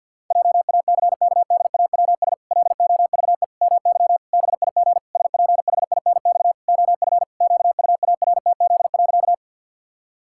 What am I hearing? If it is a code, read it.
Morse code, heard as a keyed tone, about 52 wpm, 700 Hz. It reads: JACKDAWS LOVE MY BIG SPHINX OF QUARTZ.
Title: Pangram Challenge v2 - 50wpm